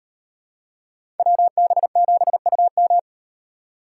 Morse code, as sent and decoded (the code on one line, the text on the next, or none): .-- -... --... ..- --
WB7UM